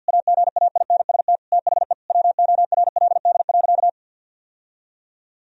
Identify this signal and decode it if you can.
Morse code, heard as a keyed tone, about 50 wpm, 700 Hz. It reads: AGAINST THE WORLD.
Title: Sentences from Top 1000 Words - Review Part 2 - 50wpm